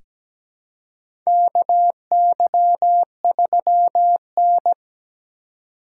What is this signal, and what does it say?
Morse code, heard as a keyed tone, about 17 wpm, 700 Hz. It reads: KY3N
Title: USA Call Signs - 4 Characters - 4x Character Spacing - 17wpm